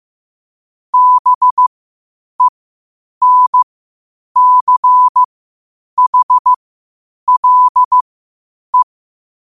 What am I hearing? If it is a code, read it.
Morse code, heard as a keyed tone, about 15 wpm, spaced as for 8 wpm, 1000 Hz. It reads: BENCHLE
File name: qotdfiles / 8wpm